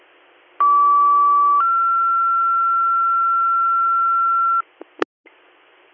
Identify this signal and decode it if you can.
Two-tone page, heard as a one-second tone, then a three-second tone, about 1150 then 1360 Hz, with noise. It is heard from 0.6 to 4.6 s.